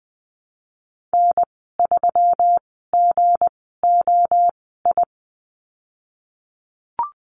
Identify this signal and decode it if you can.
Morse code, heard as a keyed tone, about 19 wpm, spaced as for 14 wpm, 700 Hz. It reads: N3GOI
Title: 186 Lesson 30g - Letter b - Call Signs 20wpm